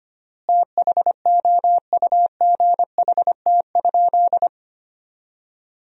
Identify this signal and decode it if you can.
Morse code, heard as a keyed tone, about 25 wpm, 700 Hz. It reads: THOUGHT?
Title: 3x Word-Spacing - Sentences from Top 300 Words - 25wpm